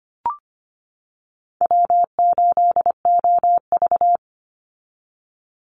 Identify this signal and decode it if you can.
Morse code, heard as a keyed tone, about 25 wpm, 700 Hz. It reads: W8OV
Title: Easy Call Signs - 25wpm